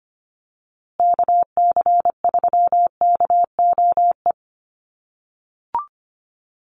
Morse code, heard as a keyed tone, about 25 wpm, 700 Hz. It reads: KC3KOE
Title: USA Call Signs Repeated 3x 25wpm